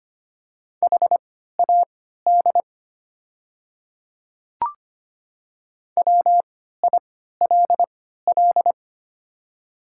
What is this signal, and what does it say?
Morse code, heard as a keyed tone, about 25 wpm, spaced as for 14 wpm, 700 Hz. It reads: HAD WILL
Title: 3x Character Spacing - Top 100 Words - 25wpm